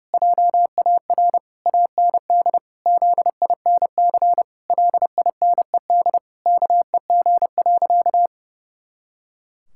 Morse code, heard as a keyed tone, about 30 wpm, 700 Hz. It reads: JAR AND ZINC LINED KEG.
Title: Pangram Challenge v2 - 30wpm